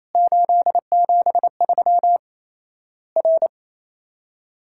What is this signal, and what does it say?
Morse code, heard as a keyed tone, about 28 wpm, 700 Hz. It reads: 873 R
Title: Worked All Europe DX Contest - QTC Exchange - 28wpm